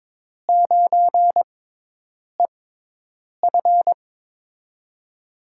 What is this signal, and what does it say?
Morse code, heard as a keyed tone, about 23 wpm, spaced as for 8 wpm, 700 Hz. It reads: 9EF
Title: USA Call Signs - 5 Characters - 6x Character Spacing - 22wpm